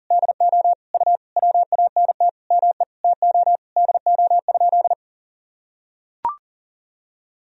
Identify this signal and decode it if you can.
Morse code, heard as a keyed tone, about 40 wpm, 700 Hz. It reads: DO U WANT ME TO DO?
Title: Sentences from Top 200 Words and Abbreviations 40wpm